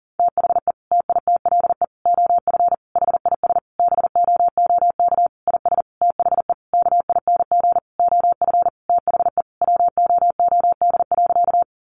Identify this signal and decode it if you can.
Morse code, heard as a keyed tone, about 40 wpm, 700 Hz. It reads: THE TITLE OF HIS BOOK IS THE KING OF THE WOOD.
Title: Sentences from Top 1200 Words 40wpm